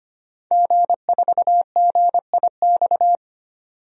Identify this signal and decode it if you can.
Morse code, heard as a keyed tone, about 25 wpm, 700 Hz. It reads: G4GIX
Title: DXCC-100 Call Signs - 5 Characters - 25wpm